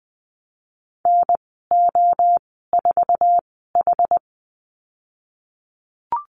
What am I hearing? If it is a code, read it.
Morse code, heard as a keyed tone, about 20 wpm, spaced as for 15 wpm, 700 Hz. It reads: NO4H
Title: 062 Lesson 11g - Letter h - Call Signs 20wpm